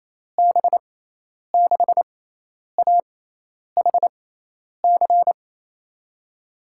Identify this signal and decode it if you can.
Morse code, heard as a keyed tone, about 27 wpm, spaced as for 10 wpm, 700 Hz. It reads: B6AHC
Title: USA Call Signs - 6 Characters - 6x Character Spacing - 28wpm